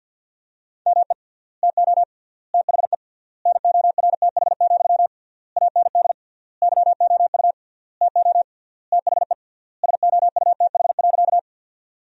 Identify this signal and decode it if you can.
Morse code, heard as a keyed tone, about 53 wpm, 700 Hz. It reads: ME TO THE NORTH, AND YOU TO THE SOUTH.
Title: 3x Word-Spacing - Sentences from Top 400 Words - 50wpm